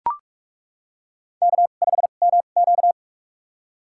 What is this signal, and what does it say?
Morse code, heard as a keyed tone, about 42 wpm, spaced as for 31 wpm, 700 Hz. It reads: K5MQ
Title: USA Call Signs - 2x Character Spacing - 45wpm